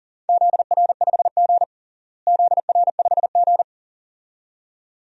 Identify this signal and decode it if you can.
Morse code, heard as a keyed tone, about 40 wpm, 700 Hz. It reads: ZR5G ZR5G
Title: DXCC-100 Call Signs - 4 Characters - Repeated 2x - 40wpm